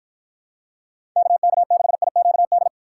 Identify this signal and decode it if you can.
Morse code, heard as a keyed tone, about 47 wpm, 700 Hz. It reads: KK6IQD